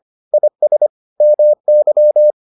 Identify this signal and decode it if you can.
Morse code, heard as a keyed tone, about 25 wpm, 600 Hz. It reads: IS MY